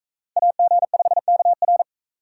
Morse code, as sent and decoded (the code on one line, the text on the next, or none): .- --. ..... -.- .-.
AG5KR